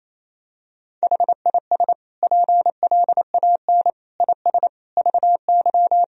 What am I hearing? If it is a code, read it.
Morse code, heard as a keyed tone, about 28 wpm, 700 Hz. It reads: HIS PLAN IS VY
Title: Sentences from Top 500 Words and Abbreviations 28wpm